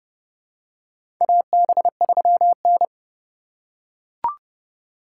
Morse code, heard as a keyed tone, about 30 wpm, 700 Hz. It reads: AB3N
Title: USA Call Signs - 4 Characters - 6x Character Spacing - 30wpm